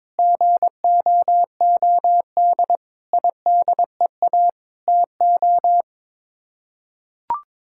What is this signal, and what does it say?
Morse code, heard as a keyed tone, about 22 wpm, 700 Hz. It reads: GOOD IDEA TO
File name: Sets of 9 Words - 22wpm